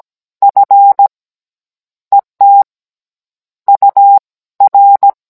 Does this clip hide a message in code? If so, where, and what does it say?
Yes: Morse code from 0.4 to 5.1 s, heard as a keyed tone, about 17 wpm, spaced as for 9 wpm, 800 Hz. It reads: FETUR